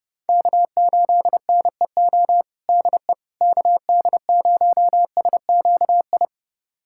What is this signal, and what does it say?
Morse code, heard as a keyed tone, about 32 wpm, 700 Hz. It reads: K8NEO DE KD0SQI
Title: USA Callsign DE Callsign 30wpm